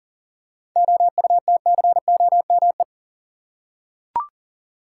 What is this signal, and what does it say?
Morse code, heard as a keyed tone, about 40 wpm, 700 Hz. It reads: OUTCOME
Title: Seven Letter Words - 40wpm